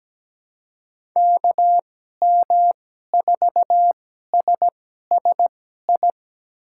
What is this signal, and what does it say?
Morse code, heard as a keyed tone, about 17 wpm, spaced as for 12 wpm, 700 Hz. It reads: KM4SSI
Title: USA Call Signs - 6 Characters - 2x Character Spacing - 17wpm